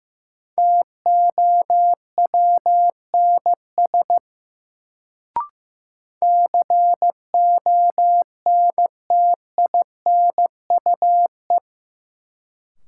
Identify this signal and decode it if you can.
Morse code, heard as a keyed tone, about 15 wpm, 700 Hz. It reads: TOWNS CONTINUE
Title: Sentences from Top 1000 Words - Review Part 6 - New Word Drill - 15wpm